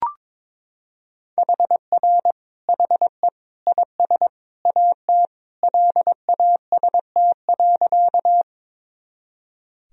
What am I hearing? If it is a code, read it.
Morse code, heard as a keyed tone, about 22 wpm, 700 Hz. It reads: HR HE IS AT LAST.
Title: Sentences from Top 300 Words and Abbreviations 22wpm